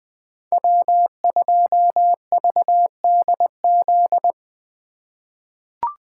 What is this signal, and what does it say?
Morse code, heard as a keyed tone, about 21 wpm, 700 Hz. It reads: W2VDZ
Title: 2023 POTA Hunter Call Signs - 20wpm